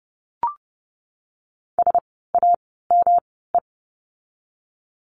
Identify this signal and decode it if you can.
Morse code, heard as a keyed tone, about 30 wpm, spaced as for 17 wpm, 700 Hz. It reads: SAME
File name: course / 30:10wpm / 237 Lesson 38f - Letter q - Words 30wpm